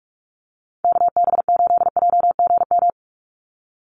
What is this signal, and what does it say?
Morse code, heard as a keyed tone, about 44 wpm, 700 Hz. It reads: KB8JGM
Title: USA Call Signs 45wpm